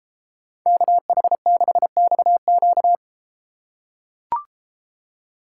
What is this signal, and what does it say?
Morse code, heard as a keyed tone, about 33 wpm, 700 Hz. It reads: KH6XQ